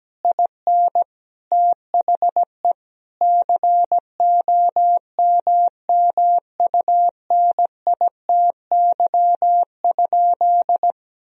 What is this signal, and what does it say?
Morse code, heard as a keyed tone, about 17 wpm, 700 Hz. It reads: IN THE COMMUNITY?
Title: Sentences from Top 1000 Words - Review Part 4 - 17wpm